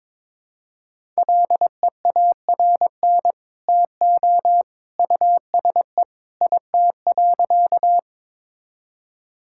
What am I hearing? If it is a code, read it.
Morse code, heard as a keyed tone, about 21 wpm, 700 Hz. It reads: LEARN TO USE IT.
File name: practice / 22wpm / Sentences with 2-Word Phrases - 22wpm